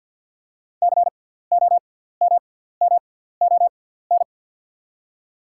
Morse code, heard as a keyed tone, about 48 wpm, spaced as for 17 wpm, 700 Hz. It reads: COMMON
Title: 6x Character Spacing - Top 500 Words - 50wpm